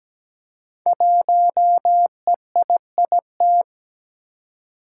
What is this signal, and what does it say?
Morse code, heard as a keyed tone, about 17 wpm, 700 Hz. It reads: 1EIIT
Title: Five Challenging Letter-Number Combinations 17wpm